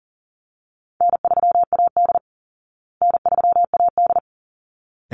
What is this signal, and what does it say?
Morse code, heard as a keyed tone, about 40 wpm, 700 Hz. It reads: N3AD N3AD
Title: Call Signs Repeated 3x - 40wpm